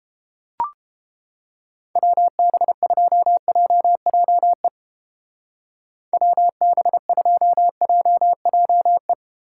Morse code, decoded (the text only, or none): WB2JJE WB2JJE